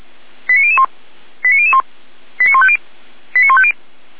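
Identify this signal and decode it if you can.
Five-tone selective call (ZVEI), heard as quick runs of five tones, tones of about 70 ms. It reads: 89001 89001 89150 89150
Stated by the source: ZVEI five-tone call received by SDR.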